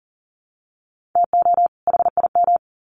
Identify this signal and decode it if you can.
Morse code, heard as a keyed tone, about 40 wpm, 700 Hz. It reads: TO HIM
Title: Sets of 2 Words 40wpm